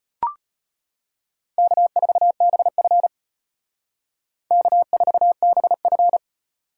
Morse code, heard as a keyed tone, about 32 wpm, 700 Hz. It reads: K4BF K4BF